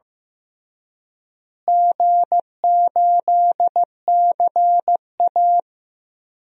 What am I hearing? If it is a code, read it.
Morse code, heard as a keyed tone, about 15 wpm, 700 Hz. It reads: G8CA
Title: DXCC-100 Call Signs - 4 Characters - Repeated 2x - 15wpm